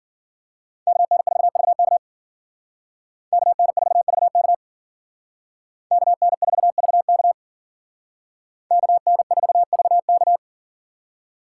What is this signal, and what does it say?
Morse code, heard as a keyed tone, about 52 wpm, 700 Hz. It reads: KN4VK KN4VK KN4VK KN4VK